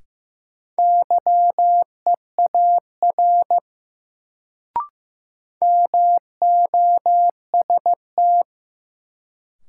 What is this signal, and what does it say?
Morse code, heard as a keyed tone, about 16 wpm, 700 Hz. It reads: YEAR MOST
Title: Sentences from Top 200 Words - New Word Drill - 15wpm